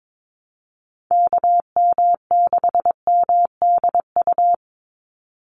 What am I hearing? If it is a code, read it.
Morse code, heard as a keyed tone, about 22 wpm, 700 Hz. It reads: KM6MDU
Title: USA Call Signs 22wpm